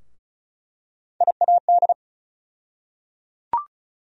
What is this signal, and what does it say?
Morse code, heard as a keyed tone, about 36 wpm, 700 Hz. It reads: IAD
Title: USA Call Sign Postfixes - 3 Letters 35wpm